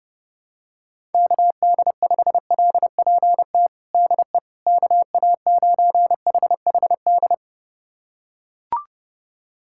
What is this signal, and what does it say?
Morse code, heard as a keyed tone, about 32 wpm, 700 Hz. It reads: KD5LPT DE KA9HHD
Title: USA Callsign DE Callsign 30wpm